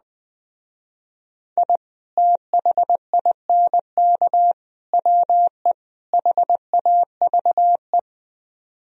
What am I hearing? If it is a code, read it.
Morse code, heard as a keyed tone, about 20 wpm, 700 Hz. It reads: I THINK WE HAVE
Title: Speed-Racing - Sets of 4 Words 17wpm